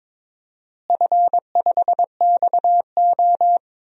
Morse code, heard as a keyed tone, about 22 wpm, 700 Hz. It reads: F5XO
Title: DXCC-100 Call Signs - 4 Characters - Repeated 2x - 22wpm